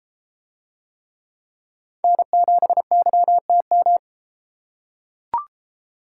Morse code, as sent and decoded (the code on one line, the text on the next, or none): -. --... -.-- - --
N7YTM